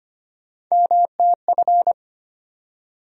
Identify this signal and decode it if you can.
Morse code, heard as a keyed tone, about 25 wpm, 700 Hz. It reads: MTF
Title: USA Call Sign Postfixes - 3 Letters 25wpm